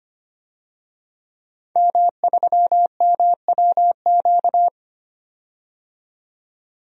Morse code, heard as a keyed tone, about 25 wpm, 700 Hz. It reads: M3MWQ
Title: DXCC-100 Call Signs - 5 Characters - 25wpm